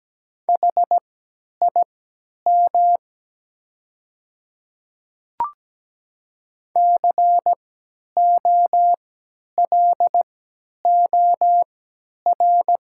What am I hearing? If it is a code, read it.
Morse code, heard as a keyed tone, about 17 wpm, spaced as for 10 wpm, 700 Hz. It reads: HIM COLOR
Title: 3x Character Spacing - Top 300 Words - 17wpm